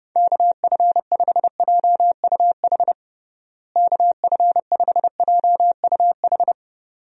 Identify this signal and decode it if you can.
Morse code, heard as a keyed tone, about 32 wpm, 700 Hz. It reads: KF5JUH KF5JUH
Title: USA Call Signs - 6 Characters - Repeated 2x - 30wpm